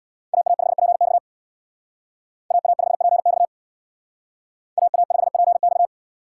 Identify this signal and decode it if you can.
Morse code, heard as a keyed tone, about 60 wpm, 700 Hz. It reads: AA5PX AA5PX AA5PX
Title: Speed Racing - USA 2x2 Call Signs 50wpm